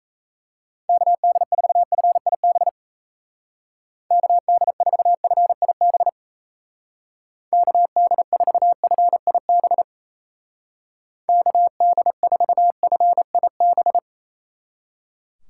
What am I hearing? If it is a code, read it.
Morse code, heard as a keyed tone, about 34 wpm, 700 Hz. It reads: KD4FIB KD4FIB KD4FIB KD4FIB
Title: Speed Racing - USA 2x3 Call Signs 28wpm